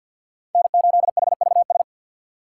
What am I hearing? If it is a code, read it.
Morse code, heard as a keyed tone, about 48 wpm, 700 Hz. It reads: N8HVS
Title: USA Call Signs - 5 Characters - 3x Character Spacing - 50wpm